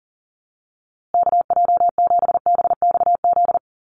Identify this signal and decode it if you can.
Morse code, heard as a keyed tone, about 40 wpm, 700 Hz. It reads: KJ7BXZ
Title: USA Call Signs Repeated 3x 40wpm